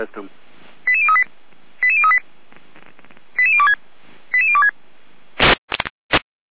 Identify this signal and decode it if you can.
Five-tone selective call (ZVEI), heard as quick runs of five tones, tones of about 70 ms, with speech. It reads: 88038 88038 88026 88026